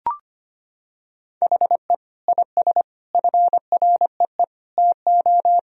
Morse code, heard as a keyed tone, about 25 wpm, 700 Hz. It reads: HE IS FREE TO
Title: Sentences from Top 1000 Words - Review Part 4 - 25wpm